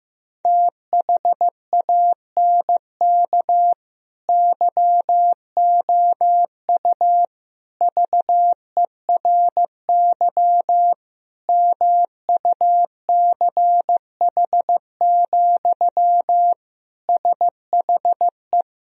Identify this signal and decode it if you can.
Morse code, heard as a keyed tone, about 15 wpm, 700 Hz. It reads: THANK YOU VERY MUCH, SHE
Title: Sentences from Top 1000 Words - Review Part 4 - 15wpm